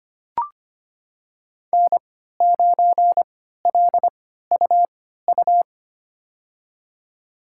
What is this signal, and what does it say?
Morse code, heard as a keyed tone, about 25 wpm, spaced as for 14 wpm, 700 Hz. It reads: N9LUU
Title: USA Call Signs - 5 Characters - 3x Character Spacing - 25wpm